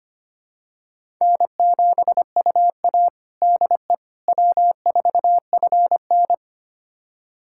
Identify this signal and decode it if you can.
Morse code, heard as a keyed tone, about 25 wpm, 700 Hz. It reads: N7UA DE W4FN
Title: USA Callsign DE Callsign - 4 Characters - 25wpm